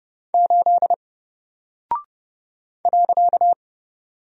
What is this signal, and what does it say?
Morse code, heard as a keyed tone, about 30 wpm, 700 Hz. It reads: 8 .
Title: Single Character (Letter:Number:Punctuation:Prosign) 30wpm